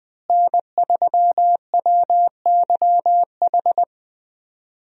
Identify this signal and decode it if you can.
Morse code, heard as a keyed tone, about 20 wpm, 700 Hz. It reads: N3WYH
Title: USA Call Signs - 5 Characters - Repeated 3x - 20wpm